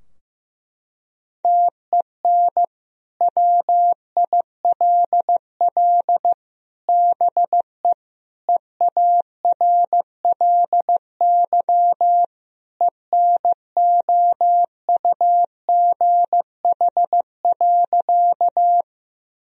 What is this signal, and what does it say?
Morse code, heard as a keyed tone, about 15 wpm, 700 Hz. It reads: TEN WILL BE EARLY ENOUGH.
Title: Sentences from Top 1000 Words - Review Part 2 - 15wpm